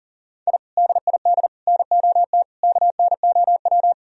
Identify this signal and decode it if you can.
Morse code, heard as a keyed tone, about 52 wpm, 700 Hz. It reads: I DID NOT KNOW